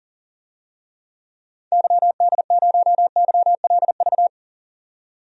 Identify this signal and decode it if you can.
Morse code, heard as a keyed tone, about 40 wpm, 700 Hz. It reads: YD0YLV
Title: DXCC-100 Call Signs - 6 Characters - 40wpm